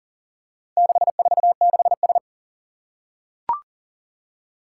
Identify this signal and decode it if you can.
Morse code, heard as a keyed tone, about 40 wpm, 700 Hz. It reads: 646S
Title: Four Challenging Letter-Number Combinations 40wpm